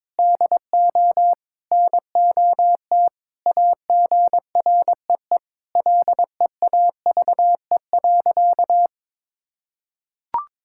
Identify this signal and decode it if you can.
Morse code, heard as a keyed tone, about 22 wpm, 700 Hz. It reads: DO NOT AGREE LEAVE.
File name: practice / 22wpm / Sentences with 4-Word Phrases - 22wpm